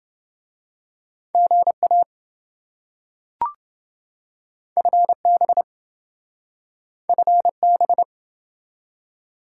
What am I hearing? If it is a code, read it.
Morse code, heard as a keyed tone, about 27 wpm, 700 Hz. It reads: GA FB FB